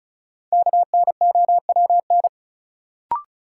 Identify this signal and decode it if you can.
Morse code, heard as a keyed tone, about 34 wpm, 700 Hz. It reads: KNOWN
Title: Sentences from Top 600 Words - New Word Drill - 35wpm